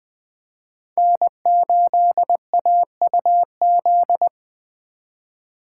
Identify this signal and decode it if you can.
Morse code, heard as a keyed tone, about 20 wpm, 700 Hz. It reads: N8AUZ
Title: USA Call Signs - 5 Characters - 3x Character Spacing - 20wpm